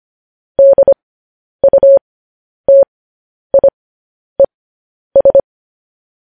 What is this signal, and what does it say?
Morse code, heard as a keyed tone, about 25 wpm, spaced as for 10 wpm, 550 Hz. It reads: DUTIES
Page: news-headlines-in-morse-code-at-10-wpm